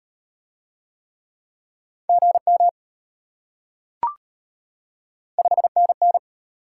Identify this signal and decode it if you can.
Morse code, heard as a keyed tone, about 33 wpm, 700 Hz. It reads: GM 5NN